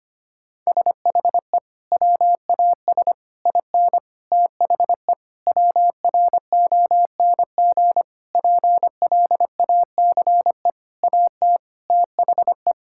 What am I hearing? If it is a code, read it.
Morse code, heard as a keyed tone, about 25 wpm, 700 Hz. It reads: SHE WAS IN THE WRONG PLACE AT THE
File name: practice / 25wpm / Sentences with 9-Word Phrases - 25wpm